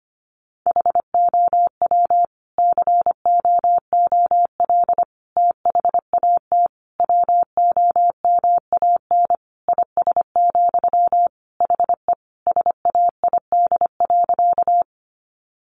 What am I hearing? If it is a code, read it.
Morse code, heard as a keyed tone, about 25 wpm, 700 Hz. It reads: HOW COOL THAT WOMAN IS, HE SAID.
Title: Sentences from Top 1200 Words 25wpm